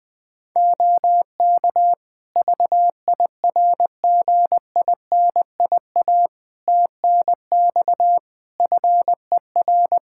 Morse code, heard as a keyed tone, about 21 wpm, 700 Hz. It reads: OK VIRGINIA TNX FER
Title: Practice QSO - Exchange 4 - Ending - 20wpm